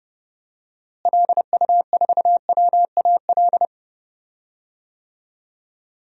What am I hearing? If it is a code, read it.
Morse code, heard as a keyed tone, about 30 wpm, 700 Hz. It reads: LU4WAL
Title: DXCC-100 Call Signs - 6 Characters - Repeated 3x - 30wpm